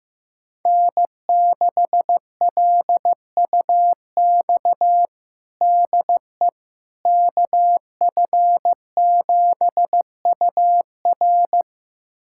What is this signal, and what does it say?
Morse code, heard as a keyed tone, about 16 wpm, 700 Hz. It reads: N6LUX DE KF7UR
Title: USA Callsign DE Callsign - 5 Characters - 15wpm